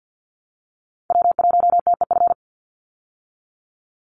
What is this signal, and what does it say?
Morse code, heard as a keyed tone, about 50 wpm, 700 Hz. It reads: W1TEF
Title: Call Signs Repeated 2x - 50wpm